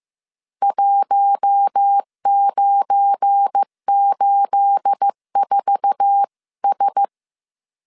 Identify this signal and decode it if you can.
Morse code, heard as a keyed tone, about 15 wpm, 800 Hz. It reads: 1984S
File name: CW / DET / 1 9 1500C S C 14wpm 3245